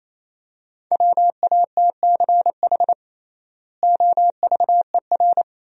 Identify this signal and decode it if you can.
Morse code, heard as a keyed tone, about 27 wpm, 700 Hz. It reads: WATCH OVER
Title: 3x Word-Spacing - Sentences from Top 300 Words - 28wpm